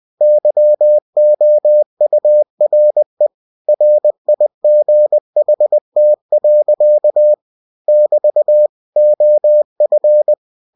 Morse code, heard as a keyed tone, about 20 wpm, 600 Hz. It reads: YOURE RIGHT. <BT> OF
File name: cw-books / winnie-the-pooh / winnie-20wpm-page-0027